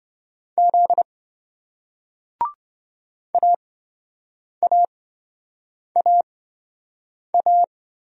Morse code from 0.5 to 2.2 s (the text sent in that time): Z